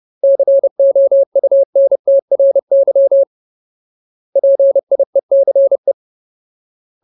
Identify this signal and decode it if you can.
Morse code, heard as a keyed tone, about 30 wpm, 550 Hz. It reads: COUNTRY PIECE
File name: Ultimate Word List / 30WPM Ultimate